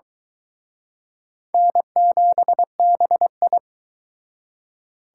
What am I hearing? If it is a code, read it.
Morse code, heard as a keyed tone, about 22 wpm, 700 Hz. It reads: N7BI